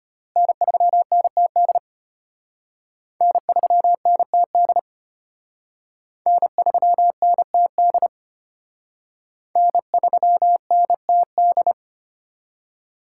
Morse code, heard as a keyed tone, about 31 wpm, 700 Hz. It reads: N3NTD N3NTD N3NTD N3NTD
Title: Speed Racing - USA 1x3 Call Signs 25wpm